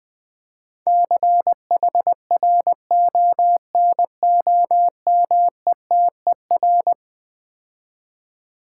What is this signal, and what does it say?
Morse code, heard as a keyed tone, about 20 wpm, 700 Hz. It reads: CHRONOMETER
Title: Difficult Words - Speed-Racing with Hints - 20wpm